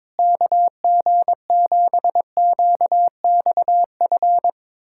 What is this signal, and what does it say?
Morse code, heard as a keyed tone, about 21 wpm, 700 Hz. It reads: KG7QXF
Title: USA Callsign DE Callsign - 6 Characters - 22wpm